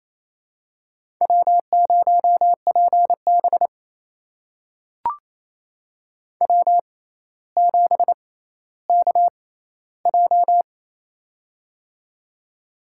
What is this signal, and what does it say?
Morse code, heard as a keyed tone, about 28 wpm, spaced as for 14 wpm, 700 Hz. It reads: W0PB W7KJ
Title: USA Call Signs - 6x Character Spacing - 28wpm